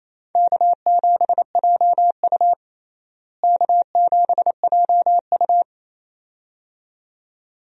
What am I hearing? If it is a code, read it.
Morse code, heard as a keyed tone, about 28 wpm, 700 Hz. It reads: K7JU K7JU
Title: USA Call Signs - 4 Characters - Repeated 2x - 28wpm